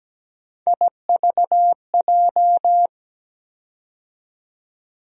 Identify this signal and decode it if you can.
Morse code, heard as a keyed tone, about 17 wpm, 700 Hz. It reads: IVJ
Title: USA Call Sign Postfixes - 3 Letters 17wpm